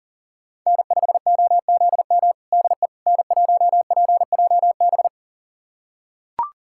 Morse code, heard as a keyed tone, about 40 wpm, 700 Hz. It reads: N5OZM DE N1PJB
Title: USA Callsign DE Callsign - 5 Characters - 40wpm